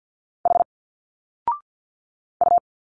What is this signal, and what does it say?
Morse code, heard as a keyed tone, about 50 wpm, 700 Hz. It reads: H U
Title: Single Letters 50wpm